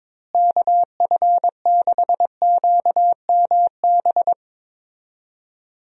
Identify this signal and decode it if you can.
Morse code, heard as a keyed tone, about 23 wpm, 700 Hz. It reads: KF6QMB
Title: USA Call Signs - 6 Characters - Repeated 3x - 22wpm